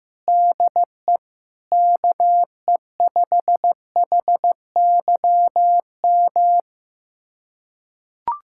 Morse code, heard as a keyed tone, about 15 wpm, 700 Hz. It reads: DE KE5HYM
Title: USA Callsign DE Callsign - 6 Characters - 15wpm